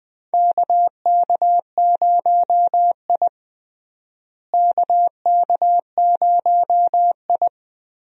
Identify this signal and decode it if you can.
Morse code, heard as a keyed tone, about 21 wpm, 700 Hz. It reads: KK0I KK0I
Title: USA Call Signs - 4 Characters - Repeated 3x - 20wpm